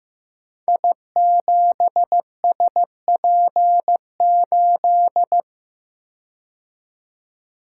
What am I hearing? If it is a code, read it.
Morse code, heard as a keyed tone, about 15 wpm, 700 Hz. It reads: I7SP8
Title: Code Groups - 5 Letter:Numbers - 15wpm